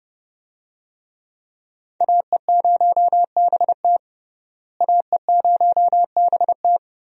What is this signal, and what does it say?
Morse code, heard as a keyed tone, about 30 wpm, 700 Hz. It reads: AE0BT AE0BT
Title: USA Call Signs - 5 Characters - Repeated 2x - 30wpm